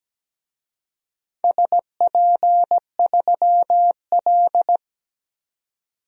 Morse code, heard as a keyed tone, about 17 wpm, 700 Hz. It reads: SP3L
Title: DXCC-100 Call Signs - 4 Characters - 17wpm